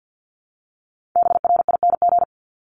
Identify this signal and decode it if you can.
Morse code, heard as a keyed tone, about 50 wpm, 700 Hz. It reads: BRING
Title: Top 500 Words 50wpm